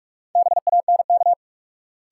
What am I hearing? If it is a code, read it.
Morse code, heard as a keyed tone, about 46 wpm, 700 Hz. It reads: BANK
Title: Sets of 5 Words - New Word Drill - 45wpm